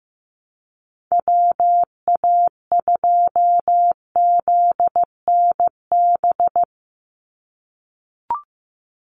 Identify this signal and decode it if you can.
Morse code, heard as a keyed tone, about 15 wpm, 700 Hz. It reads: WA2ZNB
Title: USA Call Signs 15wpm